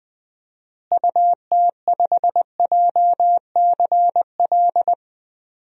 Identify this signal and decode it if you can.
Morse code, heard as a keyed tone, about 21 wpm, 700 Hz. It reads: UT5JCL
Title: DXCC-100 Call Signs - 6 Characters - Repeated 2x - 20wpm